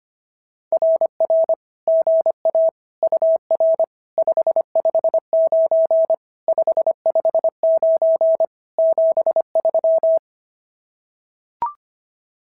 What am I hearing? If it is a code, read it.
Morse code, heard as a keyed tone, about 25 wpm, 650 Hz. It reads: RR GA UR 559 559 73
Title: SOTA QSO Practice 25wpm